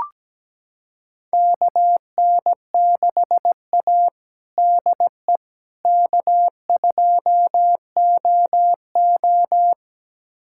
Morse code, heard as a keyed tone, about 17 wpm, 700 Hz. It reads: KN6A DE K2OO
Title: USA Callsign DE Callsign - 4 Characters - 17wpm